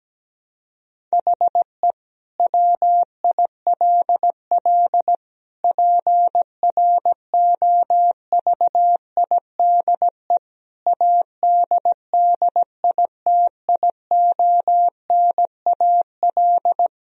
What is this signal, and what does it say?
Morse code, heard as a keyed tone, about 17 wpm, 700 Hz. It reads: HE WILL PROVIDE ADDITIONAL
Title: Sentences from Top 1000 Words - Review Part 4 - 17wpm